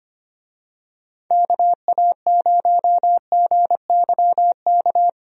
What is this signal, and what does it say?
Morse code, heard as a keyed tone, about 25 wpm, 700 Hz. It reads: KA0GYK
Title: USA Call Signs - 6 Characters - Repeated 2x - 25wpm